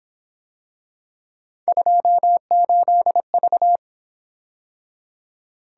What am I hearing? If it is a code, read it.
Morse code, heard as a keyed tone, about 27 wpm, 700 Hz. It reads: 28V
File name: Speed-Racing - Three Letter-Number Combinations - 22wpm